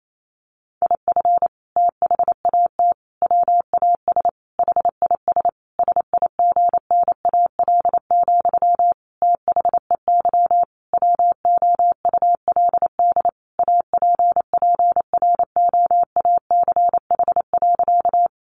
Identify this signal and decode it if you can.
Morse code, heard as a keyed tone, about 28 wpm, 700 Hz. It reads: IF THAT WAS HIS SIGNAL, THEY WOULD APPROACH.
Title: Sentences from Top 1400 Words 28wpm